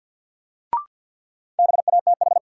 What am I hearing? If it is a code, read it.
Morse code, heard as a keyed tone, about 52 wpm, 700 Hz. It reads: BATH
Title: Sentences from Top 1600 Words - New Word Drill - 50wpm